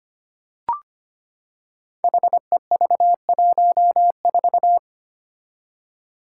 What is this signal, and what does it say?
Morse code, heard as a keyed tone, about 25 wpm, 700 Hz. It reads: HEV14
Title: Five Challenging Letter-Number Combinations 25wpm